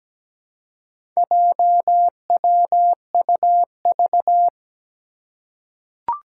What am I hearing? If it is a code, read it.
Morse code, heard as a keyed tone, about 17 wpm, 700 Hz. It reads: JWUV
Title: Four Challenging Letter-Number Combinations 17wpm